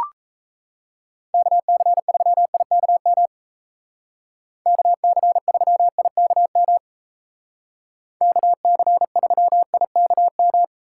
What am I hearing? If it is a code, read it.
Morse code, heard as a keyed tone, about 33 wpm, 700 Hz. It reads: KC3IKM KC3IKM KC3IKM
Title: Speed Racing - USA Call Signs 28wpm